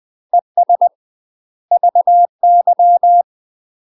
Morse code, heard as a keyed tone, about 20 wpm, 700 Hz. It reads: ES VY
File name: UK MASTER QSOs TRAINING / 20WPM / UK MASTER QSOs 0071